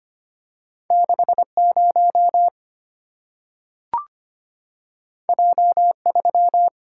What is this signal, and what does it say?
Morse code, heard as a keyed tone, about 25 wpm, 700 Hz. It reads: 60 J3
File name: Two Letter-Number Combinations - 25wpm